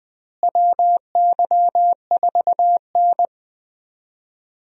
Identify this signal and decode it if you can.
Morse code, heard as a keyed tone, about 20 wpm, 700 Hz. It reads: WY4N
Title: USA Call Signs - 4 Characters - Repeated 2x - 20wpm